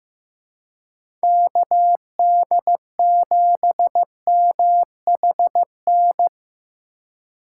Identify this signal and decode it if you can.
Morse code, heard as a keyed tone, about 15 wpm, 700 Hz. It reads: KD7MHN
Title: USA Call Signs - 6 Characters - 2x Character Spacing - 15wpm